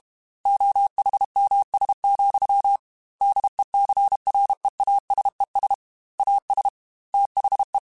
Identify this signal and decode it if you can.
Morse code, heard as a keyed tone, about 32 wpm, 800 Hz. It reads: OHMS, DECREASES AS THE
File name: Technical Subjects / Technical Subjects-mp3s / XC @30